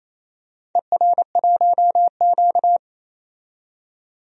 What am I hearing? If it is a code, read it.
Morse code, heard as a keyed tone, about 27 wpm, 700 Hz. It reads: ER1Q
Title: DXCC-100 Call Signs - 4 Characters - 28wpm